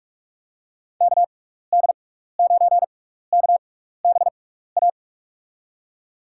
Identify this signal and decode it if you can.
Morse code, heard as a keyed tone, about 46 wpm, spaced as for 15 wpm, 700 Hz. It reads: KD9KBA
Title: USA Call Signs - 6 Characters - 6x Character Spacing - 45wpm